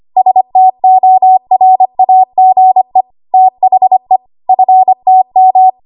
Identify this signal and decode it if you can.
Morse code, heard as a keyed tone, about 25 wpm, 750 Hz. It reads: STORAGE THE FTM
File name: Archive / 25 WPM / 25WPM